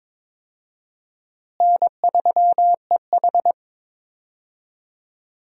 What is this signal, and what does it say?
Morse code, heard as a keyed tone, about 23 wpm, 700 Hz. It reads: N3EH
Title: USA Call Signs - 4 Characters - Repeated 3x - 22wpm